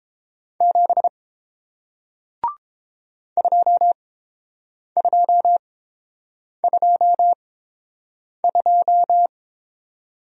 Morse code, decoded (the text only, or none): 7 2222